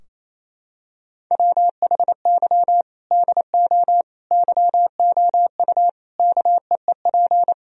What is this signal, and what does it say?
Morse code, heard as a keyed tone, about 28 wpm, 700 Hz. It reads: WHY DO YOU KEEP
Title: Sentences from Top 1000 Words - Review Part 3 - 28wpm